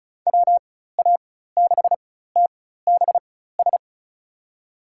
Morse code, heard as a keyed tone, about 36 wpm, spaced as for 16 wpm, 700 Hz. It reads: WA6TBS
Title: USA Call Signs - 6 Characters - 4x Character Spacing - 35wpm